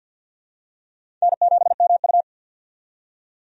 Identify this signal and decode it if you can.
Morse code, heard as a keyed tone, about 51 wpm, 700 Hz. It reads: N7MU
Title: USA Call Signs - 4 Characters - 50wpm